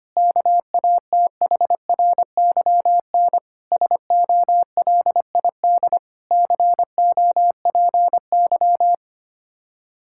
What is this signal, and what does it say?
Morse code, heard as a keyed tone, about 25 wpm, 700 Hz. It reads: KATHRYN SOLID COPY
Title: K1USN SST QSO Practice 25wpm